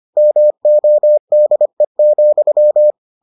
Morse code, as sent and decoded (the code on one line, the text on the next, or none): -- --- -.. --..--
MODE,